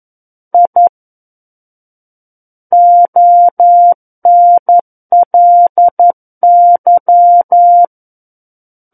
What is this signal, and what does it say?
Morse code, heard as a keyed tone, about 11 wpm, 700 Hz. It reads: I ONLY
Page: SKCC - War of the Worlds - 02-Chapter 2 - 11 WPM